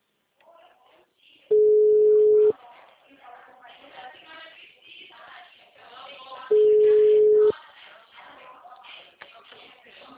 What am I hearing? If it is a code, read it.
Ringback tone (Europe, 425 Hz), with speech.